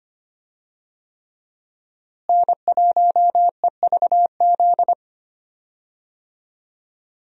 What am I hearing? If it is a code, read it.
Morse code, heard as a keyed tone, about 25 wpm, 700 Hz. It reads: N1EVZ
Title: USA Call Signs - 5 Characters - Repeated 3x - 25wpm